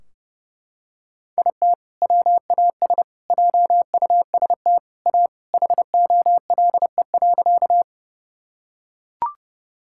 Morse code, heard as a keyed tone, about 30 wpm, 700 Hz. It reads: IT WAS JUST A HOLE.
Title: Sentences from Top 1600 Words 30wpm